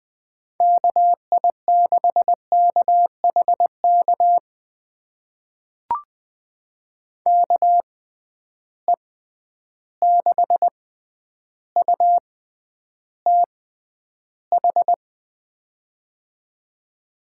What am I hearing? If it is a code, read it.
Morse code, heard as a keyed tone, about 20 wpm, spaced as for 10 wpm, 700 Hz. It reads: KI6KHK KE6UTH
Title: USA Call Signs - 6 Characters - 6x Character Spacing - 20wpm